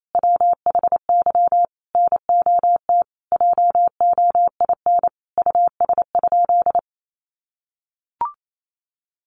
Morse code, heard as a keyed tone, about 28 wpm, 700 Hz. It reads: WHY NOT JOIN US?
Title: Sentences from Top 1400 Words 28wpm